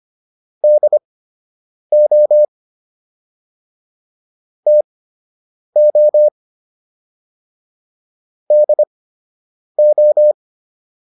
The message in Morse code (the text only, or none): DO TO DO